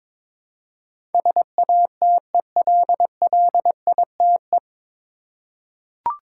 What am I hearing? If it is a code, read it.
Morse code, heard as a keyed tone, about 22 wpm, 700 Hz. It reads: SATELLITE
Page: Morse Code Ninja practice 22wpm